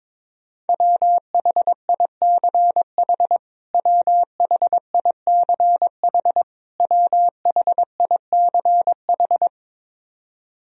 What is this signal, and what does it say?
Morse code, heard as a keyed tone, about 22 wpm, 700 Hz. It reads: WHICH WHICH WHICH
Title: Top 100 Words Repeated 22wpm